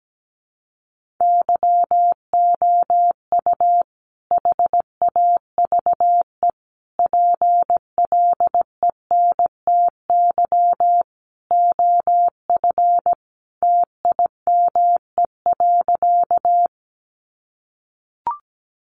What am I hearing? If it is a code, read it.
Morse code, heard as a keyed tone, about 17 wpm, 700 Hz. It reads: YOU HAVE PLENTY OF TIME.
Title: Sentences from Top 1300 Words 17wpm